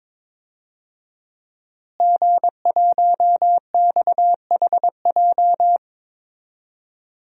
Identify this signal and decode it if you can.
Morse code, heard as a keyed tone, about 22 wpm, 700 Hz. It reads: G1XHJ
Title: DXCC-100 Call Signs - 5 Characters - Repeated 3x - 22wpm